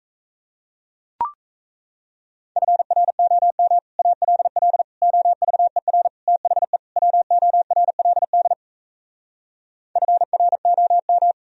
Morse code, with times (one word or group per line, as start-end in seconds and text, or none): 2.6-3.8 FROM
4.0-4.8 ALL
5.0-6.1 OVER
6.3-6.8 THE
7.0-8.5 WORLD
10.0-11.3 FROM